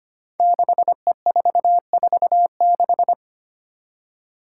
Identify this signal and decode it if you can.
Morse code, heard as a keyed tone, about 25 wpm, 700 Hz. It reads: 6E446
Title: Five Challenging Letter-Number Combinations 25wpm